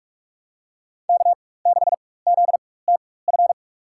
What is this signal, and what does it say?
Morse code, heard as a keyed tone, about 46 wpm, spaced as for 21 wpm, 700 Hz. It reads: K6ZTF
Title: USA Call Signs - 5 Characters - 4x Character Spacing - 45wpm